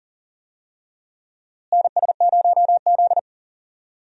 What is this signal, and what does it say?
Morse code, heard as a keyed tone, about 40 wpm, 700 Hz. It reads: NS0Z
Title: USA Call Signs - 4 Characters - Repeated 3x - 40wpm